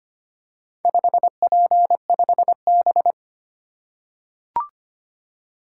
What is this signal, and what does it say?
Morse code, heard as a keyed tone, about 25 wpm, 700 Hz. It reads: 5P5B
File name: practice / 25wpm / Four Letter-Number Combinations - 25wpm